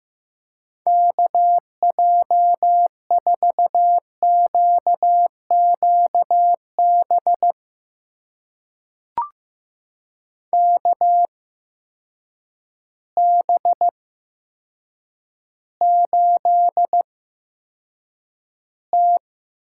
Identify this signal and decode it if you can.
Morse code, heard as a keyed tone, about 15 wpm, spaced as for 8 wpm, 700 Hz. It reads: KJ4QQB KB8T
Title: USA Call Signs - 6 Characters - 8x Character Spacing - 15wpm